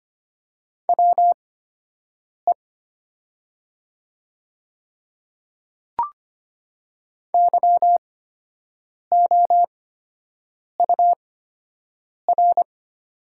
Morse code, heard as a keyed tone, about 25 wpm, spaced as for 7 wpm, 690 Hz. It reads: WE YOUR
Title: 8x Character Spacing - Top 300 Words - 25wpm